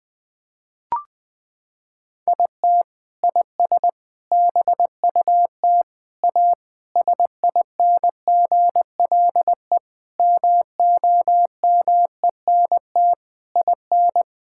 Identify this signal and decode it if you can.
Morse code, heard as a keyed tone, about 20 wpm, 700 Hz. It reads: IT IS BUT A SINGLE MOMENT IN
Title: Sentences from Top 1000 Words - Review Part 6 - 20wpm